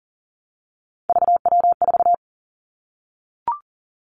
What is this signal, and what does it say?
Morse code, heard as a keyed tone, about 42 wpm, 690 Hz. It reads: VW4
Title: Three Challenging Letter-Number Combinations 40wpm